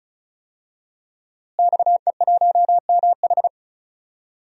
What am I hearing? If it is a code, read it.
Morse code, heard as a keyed tone, about 34 wpm, 700 Hz. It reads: XE1MH
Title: DXCC-100 Call Signs - 5 Characters - Repeated 3x - 35wpm